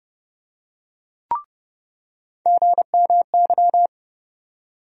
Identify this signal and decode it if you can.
Morse code, heard as a keyed tone, about 30 wpm, 700 Hz. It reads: GMY
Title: USA Call Sign Postfixes - 3 Letters 30wpm